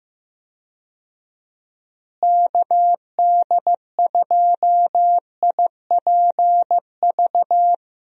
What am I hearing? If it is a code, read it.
Morse code, heard as a keyed tone, about 16 wpm, 700 Hz. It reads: KD2IPV
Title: USA Call Signs - 6 Characters - Repeated 3x - 15wpm